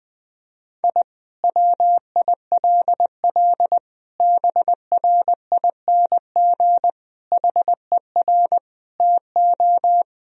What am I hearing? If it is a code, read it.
Morse code, heard as a keyed tone, about 21 wpm, 700 Hz. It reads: I WILL BRING HER TO
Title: Sentences from Top 1000 Words - Review Part 3 - 20wpm